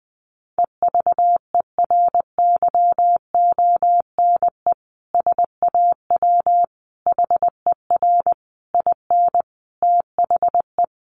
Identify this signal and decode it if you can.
Morse code, heard as a keyed tone, about 20 wpm, 700 Hz. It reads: EVERYONE SAW HER IN THE
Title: Sentences from Top 800 Words 20wpm